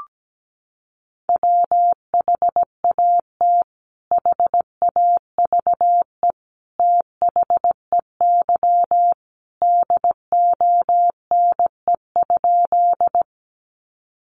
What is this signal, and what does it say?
Morse code, heard as a keyed tone, about 17 wpm, 700 Hz. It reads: WHAT HAVE THEY DONE?
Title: Sentences from Top 500 Words 17wpm